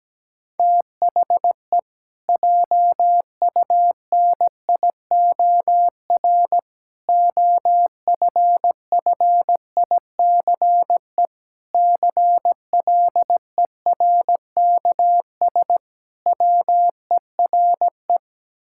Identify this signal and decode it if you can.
Morse code, heard as a keyed tone, about 17 wpm, 700 Hz. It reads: THE JUNIOR OFFICE CLERKS WERE
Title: Pangram Challenge v2 - 17wpm